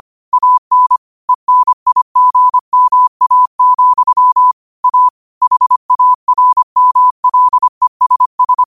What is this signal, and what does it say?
Morse code, heard as a keyed tone, about 25 wpm, 1000 Hz. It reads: AN ENIGMA, A HARMLESS
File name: 25wpm